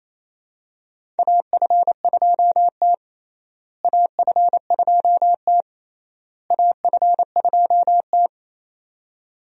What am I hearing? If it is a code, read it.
Morse code, heard as a keyed tone, about 28 wpm, 700 Hz. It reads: AF2T AF2T AF2T
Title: USA Call Signs - 4 Characters - Repeated 3x - 28wpm